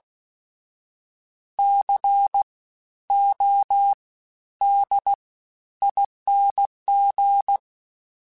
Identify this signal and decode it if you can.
Morse code, heard as a keyed tone, about 16 wpm, 800 Hz. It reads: C O D ING
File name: Special INGs-mp3s / Special INGs-mp3s / SlowToFaster ing @15